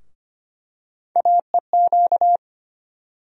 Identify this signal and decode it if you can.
Morse code, heard as a keyed tone, about 25 wpm, 700 Hz. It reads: AEQ
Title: USA Call Sign Postfixes - 3 Letters 25wpm